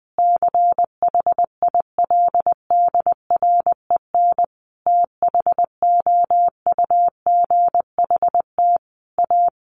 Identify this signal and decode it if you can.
Morse code, heard as a keyed tone, about 20 wpm, 700 Hz. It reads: CHILDREN THOUGHT A
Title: Sentences from Top 800 Words 20wpm